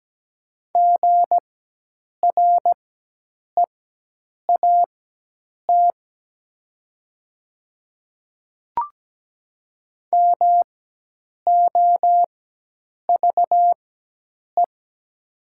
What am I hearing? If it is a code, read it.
Morse code, heard as a keyed tone, about 17 wpm, spaced as for 8 wpm, 700 Hz. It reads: GREAT MOVE
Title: 4x Character Spacing - Top 200 Words - 17wpm